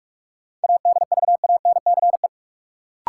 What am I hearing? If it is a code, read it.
Morse code, heard as a keyed tone, about 43 wpm, 700 Hz. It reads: ADVANCE
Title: Sentences from Top 1400 Words - New Word Drill Alt - 45wpm